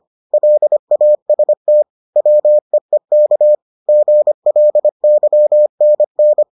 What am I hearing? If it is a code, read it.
Morse code, heard as a keyed tone, about 25 wpm, 600 Hz. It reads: LAST WEEK GLYNN